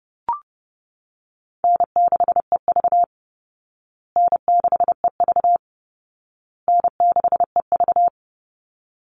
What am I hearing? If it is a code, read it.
Morse code, heard as a keyed tone, about 28 wpm, 700 Hz. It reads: N6EV N6EV N6EV
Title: Easy Call Signs Repeated 3x - 30wpm